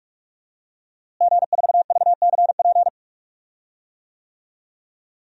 Morse code, heard as a keyed tone, about 46 wpm, 700 Hz. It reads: G4VCP
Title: DXCC-100 Call Signs - 5 Characters - Repeated 2x - 45wpm